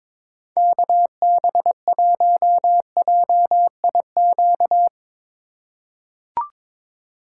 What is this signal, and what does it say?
Morse code, heard as a keyed tone, about 22 wpm, 700 Hz. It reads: KB1JIQ
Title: USA Call Signs - 6 Characters - 3x Character Spacing - 22wpm